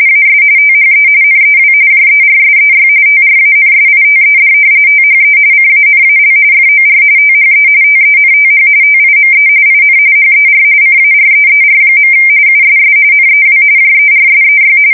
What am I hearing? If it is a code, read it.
RTTY, about 45.45 baud. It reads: AS THE NEW JERSEY TRANSITED THE CANAL FOR THE TENTH TIME IN HER CAREER. USS NEW JERSEY
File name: fsk170 / battleshipnj